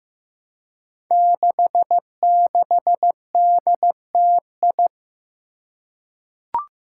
Morse code, heard as a keyed tone, about 15 wpm, 700 Hz. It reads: 66DTI
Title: Five Challenging Letter-Number Combinations 15wpm